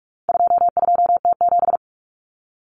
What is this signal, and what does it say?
Morse code, heard as a keyed tone, about 44 wpm, 700 Hz. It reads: 22T7
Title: Common Cut Numbers - 4-Digit 45wpm